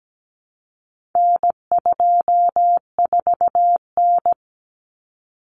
Morse code, heard as a keyed tone, about 17 wpm, 700 Hz. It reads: N24N
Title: Common Cut Numbers - 4-Digit 17wpm